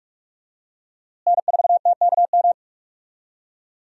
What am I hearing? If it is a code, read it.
Morse code, heard as a keyed tone, about 41 wpm, 700 Hz. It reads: N4TKM
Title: USA Call Signs - 5 Characters - Repeated 3x - 45wpm